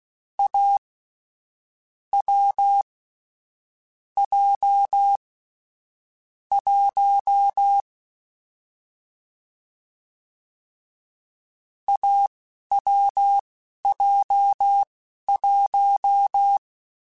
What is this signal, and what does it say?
Morse code, heard as a keyed tone, about 16 wpm, spaced as for 11 wpm, 800 Hz. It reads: A W J 1 AWJ1